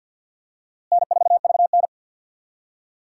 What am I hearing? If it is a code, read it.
Morse code, heard as a keyed tone, about 50 wpm, 700 Hz. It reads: N4VN
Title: USA Call Signs - 4 Characters - Repeated 2x - 50wpm